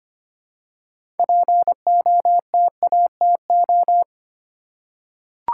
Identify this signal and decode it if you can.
Morse code, heard as a keyed tone, about 25 wpm, 700 Hz. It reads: POTATO
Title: Six Letter Words - 25wpm